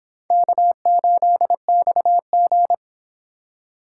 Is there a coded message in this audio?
Yes: Morse code from 0.3 to 2.7 s, heard as a keyed tone, about 25 wpm, 700 Hz. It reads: K8XG